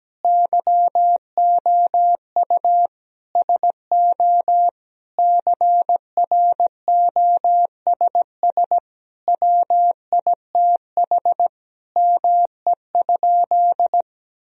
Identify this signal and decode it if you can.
Morse code, heard as a keyed tone, about 17 wpm, 700 Hz. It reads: YOU SO CROSS WITH ME?
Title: Sentences from Top 1000 Words - Review Part 1 - 17wpm